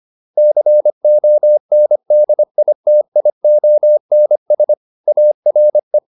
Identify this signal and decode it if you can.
Morse code, heard as a keyed tone, about 25 wpm, 600 Hz. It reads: CONDITIONS ARE
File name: QSO407 25